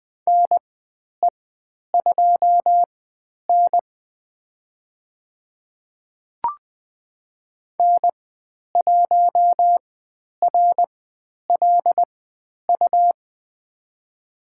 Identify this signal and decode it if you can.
Morse code, heard as a keyed tone, about 20 wpm, spaced as for 10 wpm, 700 Hz. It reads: NE2N N1RLU
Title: 102 Lesson 17g - Letter u - Call Signs 20wpm